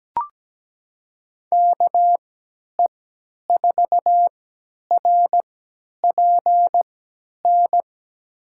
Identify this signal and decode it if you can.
Morse code, heard as a keyed tone, about 17 wpm, spaced as for 10 wpm, 700 Hz. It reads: KE4RPN
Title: USA Call Signs - 6 Characters - 3x Character Spacing - 17wpm